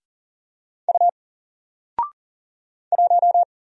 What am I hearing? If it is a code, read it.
Morse code, heard as a keyed tone, about 40 wpm, 700 Hz. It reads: U 1
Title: Single Letter-Number 40wpm